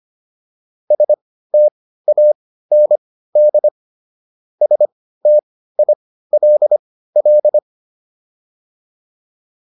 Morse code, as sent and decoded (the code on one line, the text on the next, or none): ... - .- -. -.. / ... - .. .-.. .-..
STAND STILL